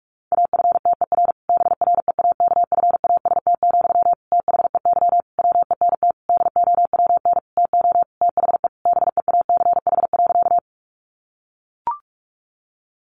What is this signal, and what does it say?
Morse code, heard as a keyed tone, about 45 wpm, 700 Hz. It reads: AFTER BREAKFAST, THEY WENT DOWN TO THE BEACH.
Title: Sentences from Top 1500 Words 45wpm